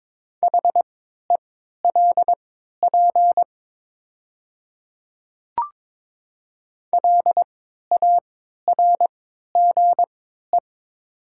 Morse code, heard as a keyed tone, about 22 wpm, spaced as for 13 wpm, 700 Hz. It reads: HELP LARGE
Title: 3x Character Spacing - Top 500 Words - 22wpm